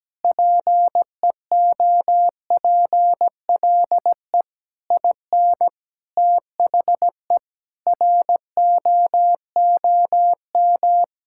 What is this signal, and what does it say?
Morse code, heard as a keyed tone, about 17 wpm, 700 Hz. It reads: PEOPLE IN THE ROOM
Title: Sentences from Top 1000 Words - Review Part 6 - 17wpm